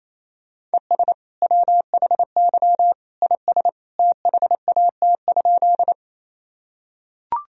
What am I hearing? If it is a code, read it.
Morse code, heard as a keyed tone, about 27 wpm, 700 Hz. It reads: ES WHY IS THAT?
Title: Sentences from Top 200 Words and Abbreviations 28wpm